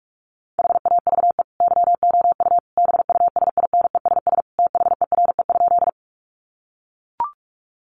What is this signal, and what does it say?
Morse code, heard as a keyed tone, about 44 wpm, 700 Hz. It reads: HAVE YOU BUSINESS THERE?
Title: Sentences from Top 600 Words 45wpm